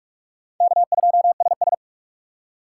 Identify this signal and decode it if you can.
Morse code, heard as a keyed tone, about 44 wpm, 700 Hz. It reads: K2SS